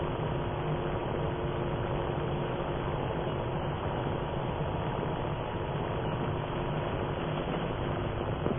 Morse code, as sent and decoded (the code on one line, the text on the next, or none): none